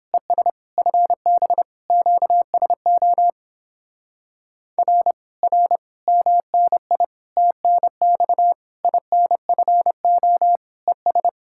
Morse code, heard as a keyed tone, about 31 wpm, 700 Hz. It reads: ES FB QSO R R MNI TNX INFO ES